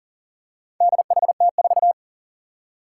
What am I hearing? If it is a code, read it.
Morse code, heard as a keyed tone, about 40 wpm, 700 Hz. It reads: DHT4
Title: Four Challenging Letter-Number Combinations 40wpm